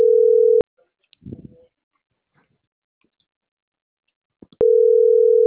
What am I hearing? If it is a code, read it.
Ringback tone (North America), 440 + 480 Hz.